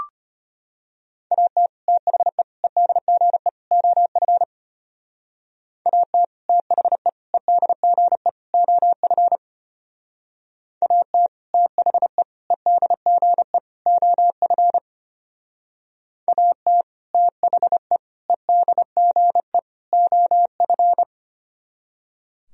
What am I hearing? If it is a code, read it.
Morse code, heard as a keyed tone, about 32 wpm, 700 Hz. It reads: AT THE EDGE OF AT THE EDGE OF AT THE EDGE OF AT THE EDGE OF